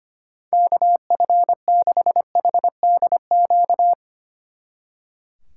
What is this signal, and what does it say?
Morse code, heard as a keyed tone, about 32 wpm, 700 Hz. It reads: KF6HDQ